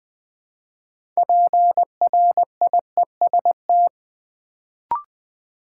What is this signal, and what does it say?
Morse code, heard as a keyed tone, about 21 wpm, 700 Hz. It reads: PRIEST
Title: Build A Word - 3 Letters At A Time - 6-Letter Words - 20wpm